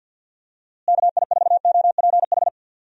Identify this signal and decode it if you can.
Morse code, heard as a keyed tone, about 48 wpm, 700 Hz. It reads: KI4OPH
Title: USA Call Signs - 6 Characters - Repeated 2x - 50wpm